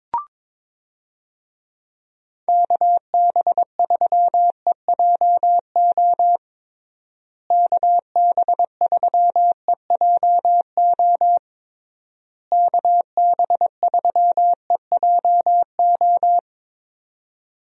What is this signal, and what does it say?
Morse code, heard as a keyed tone, about 22 wpm, 700 Hz. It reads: KB3EJO KB3EJO KB3EJO
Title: USA Call Signs - 6 Characters - Repeated 3x - 22wpm